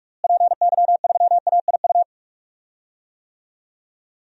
Morse code, heard as a keyed tone, about 46 wpm, 700 Hz. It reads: PY3AIU
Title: DXCC-100 Call Signs - 6 Characters - Repeated 3x - 45wpm